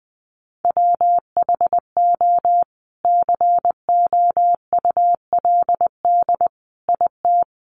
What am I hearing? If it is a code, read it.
Morse code, heard as a keyed tone, about 20 wpm, 700 Hz. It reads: WHO COULD IT
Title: Sentences from Top 1300 Words 20wpm